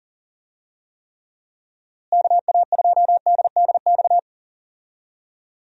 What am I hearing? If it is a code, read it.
Morse code, heard as a keyed tone, about 40 wpm, 700 Hz. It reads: KA2DDX
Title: USA Call Signs - 6 Characters - Repeated 2x - 40wpm